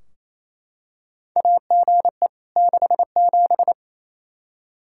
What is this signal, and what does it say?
Morse code, heard as a keyed tone, about 28 wpm, 700 Hz. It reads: AGE 67
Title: K1USN SST QSO Practice 28wpm